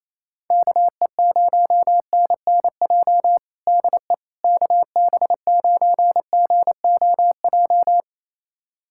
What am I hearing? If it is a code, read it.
Morse code, heard as a keyed tone, about 28 wpm, 700 Hz. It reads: KE0NNJ DE KB9GOJ
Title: USA Callsign DE Callsign - 6 Characters - 28wpm